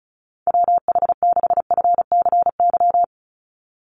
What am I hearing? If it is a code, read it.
Morse code, heard as a keyed tone, about 34 wpm, 700 Hz. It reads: WH6FCY
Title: USA Call Signs 35wpm